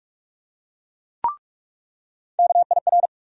Morse code, heard as a keyed tone, about 47 wpm, 700 Hz. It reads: KIR
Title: USA Call Sign Postfixes - 3 Letters 45wpm